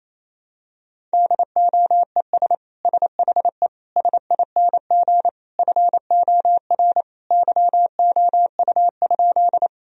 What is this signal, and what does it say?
Morse code, heard as a keyed tone, about 28 wpm, 690 Hz. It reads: DOES SHE SING FOR YOU?
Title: Sentences from Top 1000 Words - Review Part 2 - 28wpm